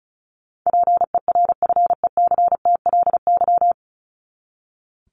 Morse code, heard as a keyed tone, about 34 wpm, 700 Hz. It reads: PERFECTLY
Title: Top 1501-1750 Words 35wpm